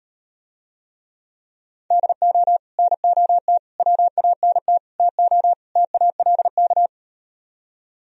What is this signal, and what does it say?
Morse code, heard as a keyed tone, about 39 wpm, 700 Hz. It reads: DO NOT WANT TO TALK